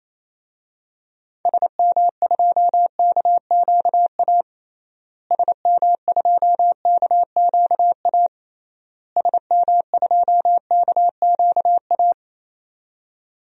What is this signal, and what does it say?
Morse code, heard as a keyed tone, about 28 wpm, 700 Hz. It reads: SM2KQA SM2KQA SM2KQA
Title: DXCC-100 Call Signs - 6 Characters - Repeated 3x - 28wpm